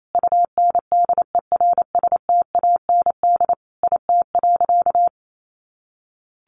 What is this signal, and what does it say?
Morse code, heard as a keyed tone, about 28 wpm, 700 Hz. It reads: UNDERSTAND IT.
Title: Sentences from Top 600 Words 28wpm